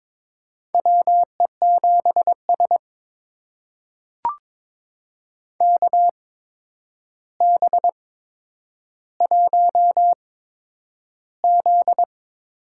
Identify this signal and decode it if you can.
Morse code, heard as a keyed tone, about 22 wpm, spaced as for 9 wpm, 700 Hz. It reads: WE7S KB1Z
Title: USA Call Signs - 4 Characters - 8x Character Spacing - 22wpm